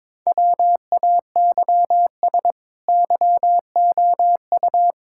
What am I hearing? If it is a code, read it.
Morse code, heard as a keyed tone, about 22 wpm, 700 Hz. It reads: WAYS YOU
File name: practice / 22wpm / Sets of 8 Words - 22wpm